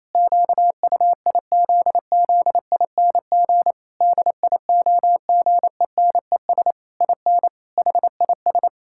Morse code, heard as a keyed tone, about 28 wpm, 700 Hz. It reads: QUIZZING DIOGENES IN HIS
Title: Pangram Challenge v2 - 28wpm